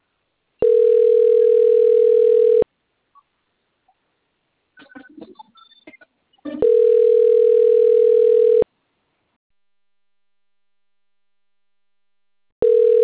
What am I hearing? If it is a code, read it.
Ringback tone (North America), 440 + 480 Hz.